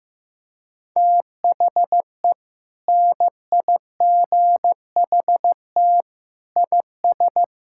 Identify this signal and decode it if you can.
Morse code, heard as a keyed tone, about 15 wpm, 700 Hz. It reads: THE NIGHT IS
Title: Sentences from Top 1600 Words 15wpm